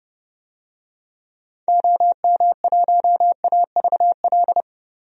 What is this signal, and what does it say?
Morse code, heard as a keyed tone, about 30 wpm, 700 Hz. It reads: OM1AVL
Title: DXCC-100 Call Signs - 6 Characters - 30wpm